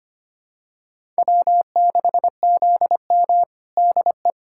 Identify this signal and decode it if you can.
Morse code, heard as a keyed tone, about 25 wpm, 700 Hz. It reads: W6ZM DE
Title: USA Callsign DE Callsign - 4 Characters - 25wpm